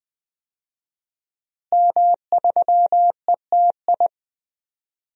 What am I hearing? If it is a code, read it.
Morse code, heard as a keyed tone, about 21 wpm, 700 Hz. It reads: M3ETI
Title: DXCC-100 Call Signs - 5 Characters - Repeated 3x - 20wpm